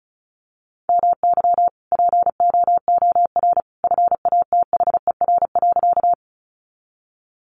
Morse code, heard as a keyed tone, about 34 wpm, 700 Hz. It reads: MY POOR FATHER.
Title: Sentences from Top 1000 Words 35wpm